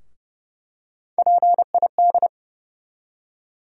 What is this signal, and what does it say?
Morse code, heard as a keyed tone, about 30 wpm, 700 Hz. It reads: PID
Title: USA Call Sign Postfixes - 3 Letters 30wpm